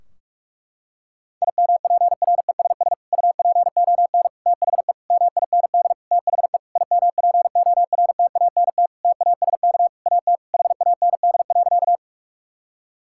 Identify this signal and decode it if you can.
Morse code, heard as a keyed tone, about 46 wpm, 700 Hz. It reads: IMPRESS UPON THE MIND THE IMPORTANT TASK AT HAND.
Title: Sentences from Top 1600 Words 45wpm